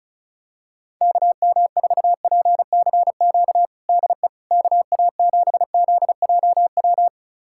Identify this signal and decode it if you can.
Morse code, heard as a keyed tone, about 36 wpm, 700 Hz. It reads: KM4PCQ DE KA7ZJW
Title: USA Callsign DE Callsign - 6 Characters - 35wpm